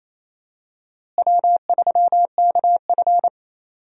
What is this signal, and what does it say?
Morse code, heard as a keyed tone, about 28 wpm, 700 Hz. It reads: W3KF
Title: USA Call Signs - 4 Characters - Repeated 3x - 28wpm